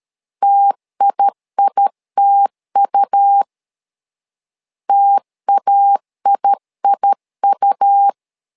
Morse code, heard as a keyed tone, about 13 wpm, 800 Hz. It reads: TIITU TAIIU
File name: LEZIONI / 2 UHDN / 2 UH 12wpm 1439